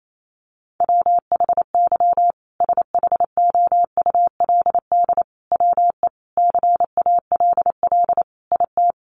Morse code, heard as a keyed tone, about 28 wpm, 700 Hz. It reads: WHY SHOULD WE CALL IT
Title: Sentences from Top 1100 Words 28wpm